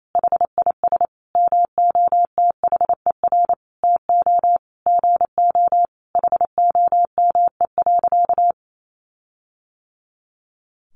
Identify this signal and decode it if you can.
Morse code, heard as a keyed tone, about 28 wpm, 700 Hz. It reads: HIS MOTHER TO GO HOME.
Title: Sentences from Top 1400 Words 28wpm